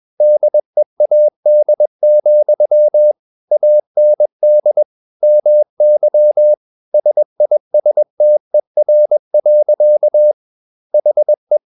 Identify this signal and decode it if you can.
Morse code, heard as a keyed tone, about 21 wpm, 600 Hz. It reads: DEAD, AND MY SISTER. HE
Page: CW med Gnister